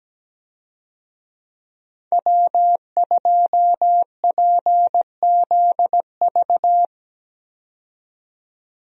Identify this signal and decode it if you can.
Morse code, heard as a keyed tone, about 17 wpm, 700 Hz. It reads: W2PZV
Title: USA Call Signs - 5 Characters - 17wpm